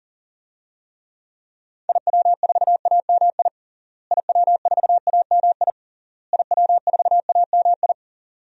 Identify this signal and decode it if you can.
Morse code, heard as a keyed tone, about 40 wpm, 700 Hz. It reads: IW4AMI IW4AMI IW4AMI
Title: DXCC-100 Call Signs - 6 Characters - Repeated 3x - 40wpm